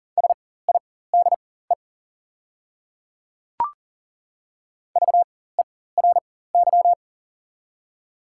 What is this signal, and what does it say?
Morse code, heard as a keyed tone, about 41 wpm, spaced as for 19 wpm, 690 Hz. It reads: SIDE VERY
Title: 4x Character Spacing - Top 200 Words - 40wpm